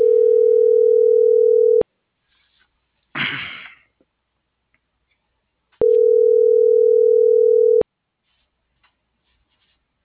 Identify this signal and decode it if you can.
Ringback tone (North America), 440 + 480 Hz.